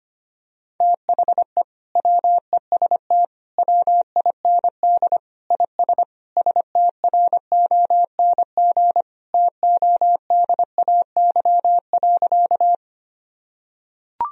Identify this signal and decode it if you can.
Morse code, heard as a keyed tone, about 25 wpm, 700 Hz. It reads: THE WEST WIND IS STRONG TODAY.
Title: Sentences from Top 1000 Words - Review Part 2 - 25wpm